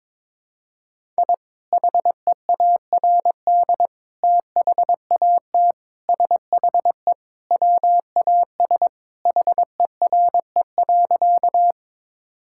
Morse code, heard as a keyed tone, about 22 wpm, 700 Hz. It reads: I HEARD THAT SHE WAS HERE.
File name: practice / 22wpm / Sentences with 3-Word Phrases - 22wpm